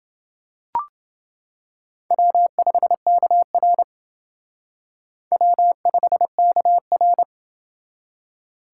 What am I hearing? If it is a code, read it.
Morse code, heard as a keyed tone, about 28 wpm, 700 Hz. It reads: W5KR W5KR